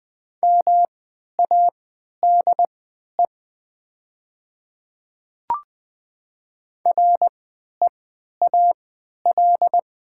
Morse code, heard as a keyed tone, about 20 wpm, spaced as for 11 wpm, 700 Hz. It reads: MADE REAL
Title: 3x Character Spacing - Top 300 Words - 20wpm